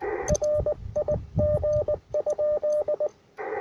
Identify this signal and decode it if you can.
Morse code, heard as a keyed tone, about 19 wpm, 600 Hz, with radio noise. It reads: RIG?
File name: attachments / UR4MCK IZ5GVP